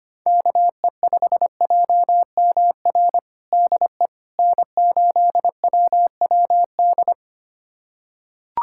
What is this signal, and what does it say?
Morse code, heard as a keyed tone, about 25 wpm, 700 Hz. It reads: KE5JMR DE N8WWD
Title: USA Callsign DE Callsign 25wpm